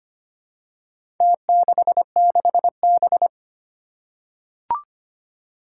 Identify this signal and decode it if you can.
Morse code, heard as a keyed tone, about 25 wpm, 700 Hz. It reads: T66B
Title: Four Challenging Letter-Number Combinations 25wpm